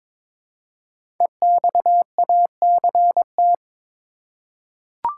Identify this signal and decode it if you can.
Morse code, heard as a keyed tone, about 22 wpm, 700 Hz. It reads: EXACT
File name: 5 Letter Words - First 3 Letters Given - 22wpm